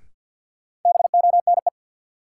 Morse code, heard as a keyed tone, about 47 wpm, 700 Hz. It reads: BONE
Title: Sentences from Top 1600 Words - New Word Drill Alt - 50wpm